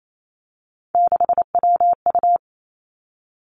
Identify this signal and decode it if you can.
Morse code, heard as a keyed tone, about 28 wpm, 700 Hz. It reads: 6WU
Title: Three Challenging Letter-Number Combinations 28wpm